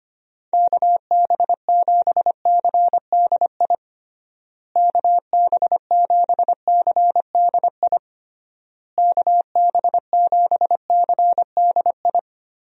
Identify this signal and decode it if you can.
Morse code, heard as a keyed tone, about 25 wpm, 700 Hz. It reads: KB7CDI KB7CDI KB7CDI
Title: USA Call Signs - 6 Characters - Repeated 3x - 25wpm